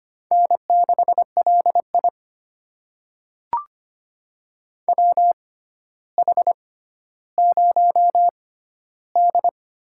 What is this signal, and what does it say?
Morse code, heard as a keyed tone, about 25 wpm, spaced as for 13 wpm, 700 Hz. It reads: N6LI WH0D
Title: USA Call Signs - 6x Character Spacing - 25wpm